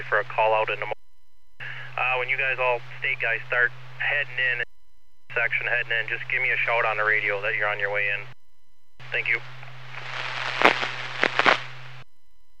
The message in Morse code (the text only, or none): none